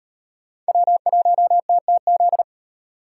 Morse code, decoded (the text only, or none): W1TTZ